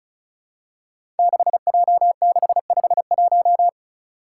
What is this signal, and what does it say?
Morse code, heard as a keyed tone, about 36 wpm, 700 Hz. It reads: 6J651
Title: Five Challenging Letter-Number Combinations 35wpm